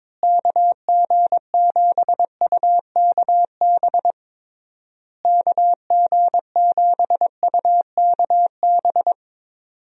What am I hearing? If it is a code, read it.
Morse code, heard as a keyed tone, about 22 wpm, 700 Hz. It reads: KG7UKB KG7UKB
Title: USA Call Signs - 6 Characters - Repeated 3x - 22wpm